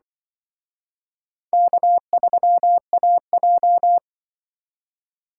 Morse code, heard as a keyed tone, about 24 wpm, 700 Hz. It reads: K3AJ
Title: Speed Racing - USA 1x2 Call Signs 20wpm